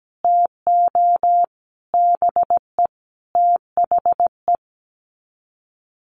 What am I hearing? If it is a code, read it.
Morse code, heard as a keyed tone, about 17 wpm, 700 Hz. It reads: TO BE THE
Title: Sets of 5 Words Encore 17wpm